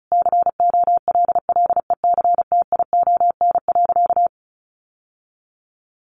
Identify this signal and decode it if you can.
Morse code, heard as a keyed tone, about 34 wpm, 700 Hz. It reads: COLLECTION.
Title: Sentences from Top 1200 Words 35wpm